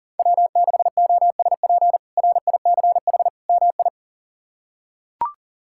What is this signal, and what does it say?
Morse code, heard as a keyed tone, about 37 wpm, 700 Hz. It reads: W6OSP RICH MI
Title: K1USN SST QSO Practice - 40wpm